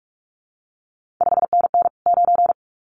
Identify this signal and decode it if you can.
Morse code, heard as a keyed tone, about 46 wpm, 700 Hz. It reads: 5NN 9
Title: CQ WW DX Contest Exchange Encore - 45wpm